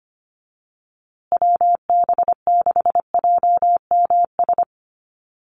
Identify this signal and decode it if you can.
Morse code, heard as a keyed tone, about 25 wpm, 700 Hz. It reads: WB6JMS
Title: USA Call Signs Repeated 2x 25wpm